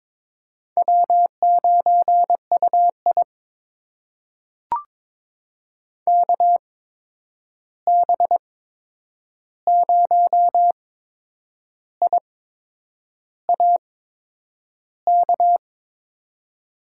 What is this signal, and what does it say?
Morse code, heard as a keyed tone, about 22 wpm, spaced as for 8 wpm, 700 Hz. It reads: W9UI KB0IAK
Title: USA Call Signs - 8x Character Spacing - 22wpm